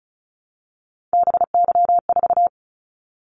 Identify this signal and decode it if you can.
Morse code, heard as a keyed tone, about 36 wpm, 700 Hz. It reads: BY4
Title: Callsign Prefixes for Top 100 DXCC - 35wpm